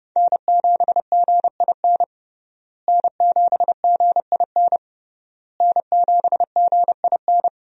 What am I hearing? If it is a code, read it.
Morse code, heard as a keyed tone, about 31 wpm, 700 Hz. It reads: N7GIN N7GIN N7GIN
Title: USA Call Signs - 5 Characters - Repeated 3x - 30wpm